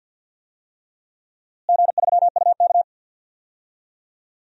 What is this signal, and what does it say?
Morse code, heard as a keyed tone, about 52 wpm, 700 Hz. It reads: G3UK
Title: DXCC-100 Call Signs - 4 Characters - Repeated 3x - 50wpm